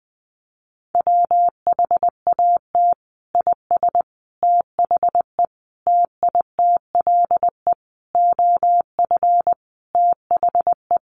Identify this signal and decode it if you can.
Morse code, heard as a keyed tone, about 20 wpm, 700 Hz. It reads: WHAT IS THE TITLE OF THE
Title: Sentences from Top 1200 Words 20wpm